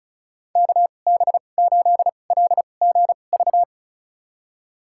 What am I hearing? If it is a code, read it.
Morse code, heard as a keyed tone, about 34 wpm, spaced as for 25 wpm, 690 Hz. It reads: KB8LGV
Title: USA Call Signs - 6 Characters - 2x Character Spacing - 35wpm